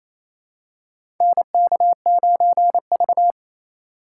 Morse code, heard as a keyed tone, about 28 wpm, 700 Hz. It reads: NK9V
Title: USA Call Signs - 3x Character Spacing - 28wpm